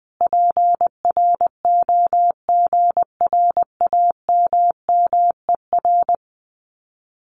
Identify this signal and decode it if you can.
Morse code, heard as a keyed tone, about 20 wpm, 700 Hz. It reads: PROGRAMMER
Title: Common QSO Phrases Encore 20wpm